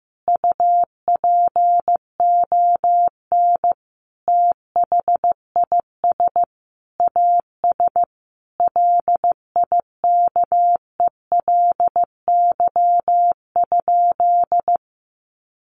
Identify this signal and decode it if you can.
Morse code, heard as a keyed tone, about 15 wpm, 700 Hz. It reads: UPON THIS AS LIKELY?
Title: Sentences from Top 1100 Words 15wpm